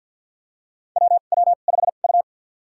Morse code, heard as a keyed tone, about 50 wpm, spaced as for 36 wpm, 700 Hz. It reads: WW5U
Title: USA Call Signs - 2x Character Spacing - 50wpm